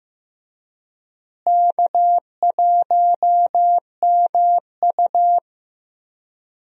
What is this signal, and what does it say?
Morse code, heard as a keyed tone, about 15 wpm, 700 Hz. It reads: K1MU
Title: USA Call Signs - 4 Characters - Repeated 3x - 15wpm